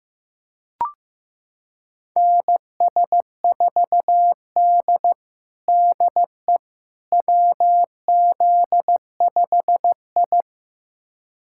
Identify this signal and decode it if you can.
Morse code, heard as a keyed tone, about 15 wpm, 700 Hz. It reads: NS4D DE WZ5I
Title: USA Callsign DE Callsign - 4 Characters - 15wpm